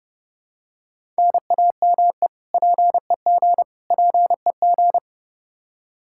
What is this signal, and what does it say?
Morse code, heard as a keyed tone, about 31 wpm, 700 Hz. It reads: NAME PEG PEG
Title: Practice QSO - Exchange 1 - Beginning - 30wpm